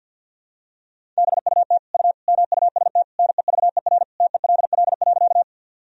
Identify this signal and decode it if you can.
Morse code, heard as a keyed tone, about 50 wpm, 700 Hz. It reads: BUT U MUST NEVER TELL.
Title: Sentences from Top 300 Words and Abbreviations 50wpm